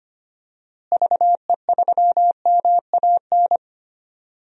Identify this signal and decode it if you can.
Morse code, heard as a keyed tone, about 25 wpm, 700 Hz. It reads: VE3MAN
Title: DXCC-100 Call Signs - 6 Characters - Repeated 2x - 25wpm